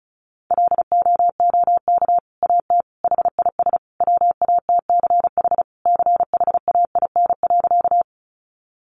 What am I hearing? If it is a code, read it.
Morse code, heard as a keyed tone, about 34 wpm, 700 Hz. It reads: LOOK AT HIS WATCH CHAIN.
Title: Sentences from Top 1500 Words 35wpm